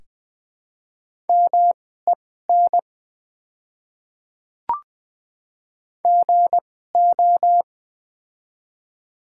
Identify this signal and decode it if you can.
Morse code, heard as a keyed tone, about 20 wpm, spaced as for 14 wpm, 700 Hz. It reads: MEN GO
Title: 2x Character Spacing - Top 200 Words - 20wpm